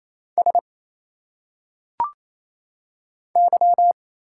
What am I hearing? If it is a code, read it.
Morse code, heard as a keyed tone, about 28 wpm, 700 Hz. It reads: S Y
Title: Single Letters - Phonetic Pronunciation 28wpm